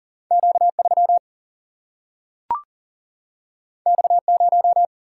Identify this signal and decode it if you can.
Morse code, heard as a keyed tone, about 40 wpm, 700 Hz. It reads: Q3 X0
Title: Single-Letter Single-Number 40wpm